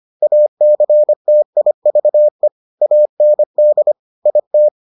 Morse code, heard as a keyed tone, about 25 wpm, 600 Hz. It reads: ACTIVE AND IT